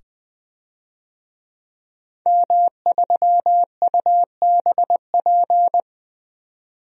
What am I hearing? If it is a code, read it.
Morse code, heard as a keyed tone, about 20 wpm, 700 Hz. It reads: M3UBP
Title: DXCC-100 Call Signs - 5 Characters - 20wpm